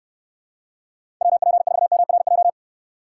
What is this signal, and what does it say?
Morse code, heard as a keyed tone, about 69 wpm, 700 Hz. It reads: WP4MRJ